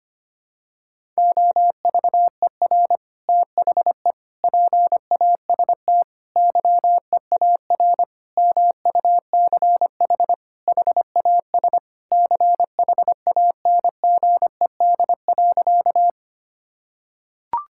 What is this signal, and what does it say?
Morse code, heard as a keyed tone, about 25 wpm, 700 Hz. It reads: OVER THE PAST YEAR MUCH HAS CHANGED.
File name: practice / 25wpm / Sentences with 3-Word Phrases - 25wpm